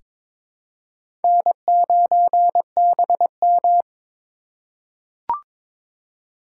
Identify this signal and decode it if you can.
Morse code, heard as a keyed tone, about 22 wpm, 700 Hz. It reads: N9BM
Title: USA Call Signs - 4x Character Spacing - 22wpm